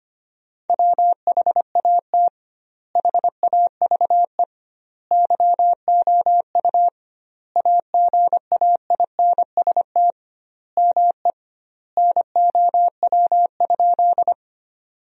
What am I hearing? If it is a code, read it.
Morse code, heard as a keyed tone, about 25 wpm, 700 Hz. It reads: WHAT HAVE YOU AGAINST ME NOW?
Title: 2x Word-Spacing - Sentences from Top 400 Words - 25wpm